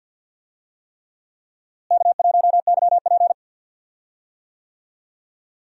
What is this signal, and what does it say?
Morse code, heard as a keyed tone, about 53 wpm, 700 Hz. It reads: K1YP
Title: USA Call Signs - 4 Characters - Repeated 3x - 50wpm